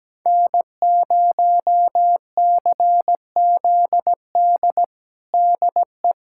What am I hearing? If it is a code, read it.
Morse code, heard as a keyed tone, about 17 wpm, 700 Hz. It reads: N0CZD DE
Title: USA Callsign DE Callsign - 5 Characters - 17wpm